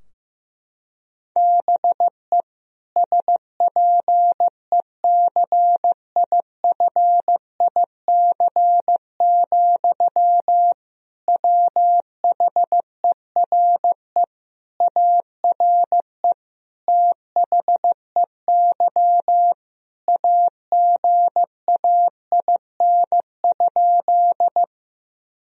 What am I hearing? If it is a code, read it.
Morse code, heard as a keyed tone, about 15 wpm, 700 Hz. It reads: BE SPECIFIC, WHERE ARE THEY AGAIN?
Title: Sentences from Top 1000 Words - Review Part 3 - 15wpm